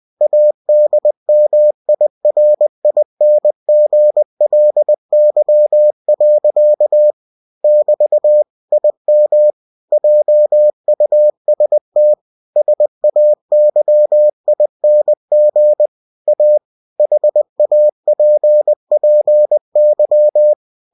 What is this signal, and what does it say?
Morse code, heard as a keyed tone, about 20 wpm, 600 Hz. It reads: ADMIRINGLY. <BT> IM JUST SAYING A HAPPY
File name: cw-books / winnie-the-pooh / winnie-20wpm-page-0052